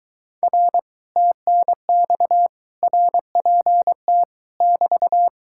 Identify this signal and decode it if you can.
Morse code, heard as a keyed tone, about 24 wpm, 700 Hz. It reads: R TNX RPT <BT>